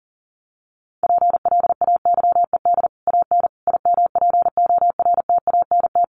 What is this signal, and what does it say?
Morse code, heard as a keyed tone, about 40 wpm, 700 Hz. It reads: PLAYED AN IMPORTANT
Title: Sets of 5 Words Encore2 40wpm